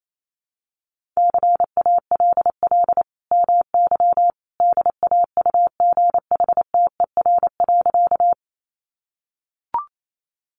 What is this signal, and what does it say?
Morse code, heard as a keyed tone, about 28 wpm, 700 Hz. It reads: CALL MY DAUGHTER.
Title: Sentences from Top 1500 Words 28wpm